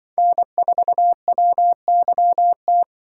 Morse code, heard as a keyed tone, about 24 wpm, 700 Hz. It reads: N4WYT
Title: Speed Racing - USA 1x3 Call Signs 20wpm